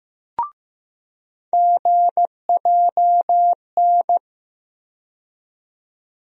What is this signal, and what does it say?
Morse code, heard as a keyed tone, about 15 wpm, 700 Hz. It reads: GJN
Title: USA Call Sign Postfixes - 3 Letters 15wpm